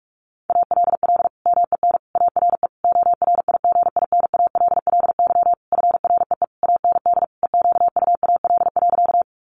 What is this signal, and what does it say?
Morse code, heard as a keyed tone, about 44 wpm, 700 Hz. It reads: ALL MEN ARE ORIGINALLY FREE AND EQUAL.
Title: Sentences from Top 1500 Words 45wpm